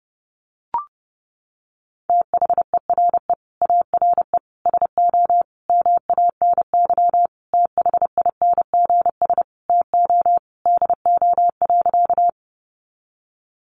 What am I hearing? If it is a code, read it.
Morse code, heard as a keyed tone, about 30 wpm, 700 Hz. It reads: THERE ARE SO MANY THINGS TO DO.
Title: Additional Easy Phrases 30wpm